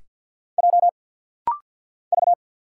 Morse code, heard as a keyed tone, about 50 wpm, 690 Hz. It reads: J V
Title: Single Letter-Number Alt - Rapid-Fire 50wpm